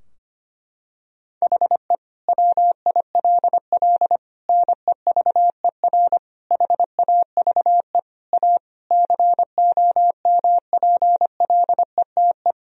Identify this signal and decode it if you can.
Morse code, heard as a keyed tone, about 25 wpm, 700 Hz. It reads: HE WILL NEVER HAVE A COMPLETE
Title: Sentences from Top 1000 Words - Review Part 2 - 25wpm